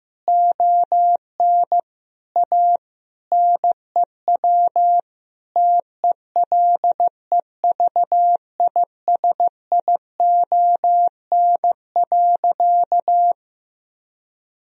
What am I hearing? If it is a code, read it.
Morse code, heard as a keyed tone, about 16 wpm, 700 Hz. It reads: ON A NEW TELEVISION.
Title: Sentences from Top 1000 Words - Review Part 4 - 15wpm